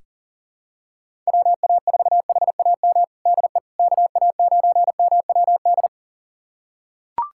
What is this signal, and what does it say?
Morse code, heard as a keyed tone, about 37 wpm, 700 Hz. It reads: WA4HAM DE KA9MWD
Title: USA Callsign DE Callsign - 6 Characters - 40wpm